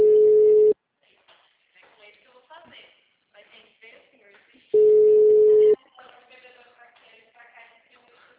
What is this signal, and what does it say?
Ringback tone (Europe, 425 Hz), with speech.